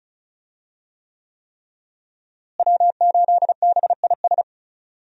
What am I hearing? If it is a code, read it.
Morse code, heard as a keyed tone, about 34 wpm, 700 Hz. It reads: W8BIS
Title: USA Call Signs - 5 Characters - Repeated 3x - 35wpm